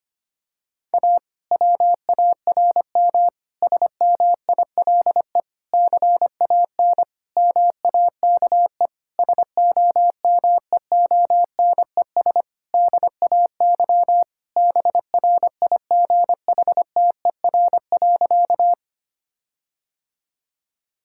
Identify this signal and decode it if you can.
Morse code, heard as a keyed tone, about 25 wpm, 700 Hz. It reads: A WARM SMILE CAN MAKE SOMEONES DAY BRIGHTER.
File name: practice / 25wpm / Difficult Sentences with Hints - 25wpm